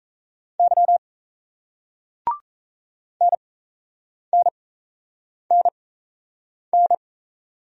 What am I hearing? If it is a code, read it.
Morse code, heard as a keyed tone, about 37 wpm, spaced as for 8 wpm, 700 Hz. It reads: Y NNNN